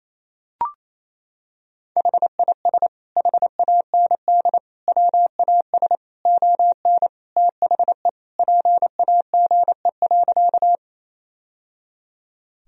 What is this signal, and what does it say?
Morse code, heard as a keyed tone, about 28 wpm, 700 Hz. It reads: HIS HAND WAS ON THE PAGE.
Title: Sentences from Top 1000 Words - Review Part 1 - 28wpm